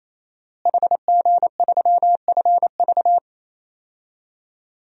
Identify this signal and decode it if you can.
Morse code, heard as a keyed tone, about 27 wpm, 700 Hz. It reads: HG3FV
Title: DXCC-100 Call Signs - 5 Characters - Repeated 3x - 28wpm